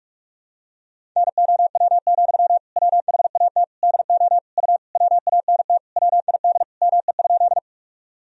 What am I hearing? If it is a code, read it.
Morse code, heard as a keyed tone, about 46 wpm, 700 Hz. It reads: NOW, WHAT DO U WANT WID ME?
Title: Sentences from Top 200 Words and Abbreviations 45wpm